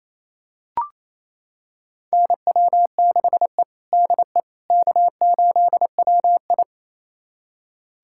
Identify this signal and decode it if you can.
Morse code, heard as a keyed tone, about 28 wpm, 700 Hz. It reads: NW6E DE K8WI
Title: USA Callsign DE Callsign - 4 Characters - 28wpm